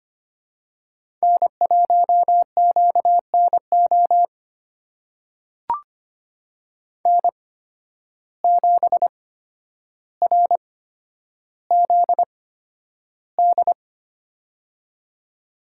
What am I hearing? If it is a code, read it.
Morse code, heard as a keyed tone, about 25 wpm, spaced as for 11 wpm, 700 Hz. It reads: N1QNO N7RZD
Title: USA Call Signs - 5 Characters - 8x Character Spacing - 25wpm